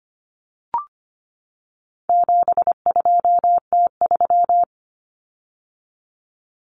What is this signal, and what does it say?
Morse code, heard as a keyed tone, about 25 wpm, 700 Hz. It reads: 72T3
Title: Common Cut Numbers - 4-Digit 25wpm